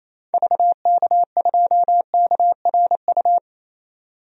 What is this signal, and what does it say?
Morse code, heard as a keyed tone, about 28 wpm, 700 Hz. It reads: VK2KRU
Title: DXCC-100 Call Signs - 6 Characters - Repeated 2x - 28wpm